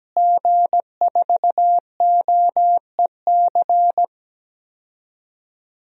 Morse code, heard as a keyed tone, about 17 wpm, 700 Hz. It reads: G4OEC
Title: DXCC-100 Call Signs - 5 Characters - Repeated 2x - 17wpm